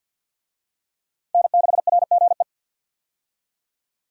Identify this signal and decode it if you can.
Morse code, heard as a keyed tone, about 51 wpm, 700 Hz. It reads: N6RGE
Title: USA Call Signs - 5 Characters - 50wpm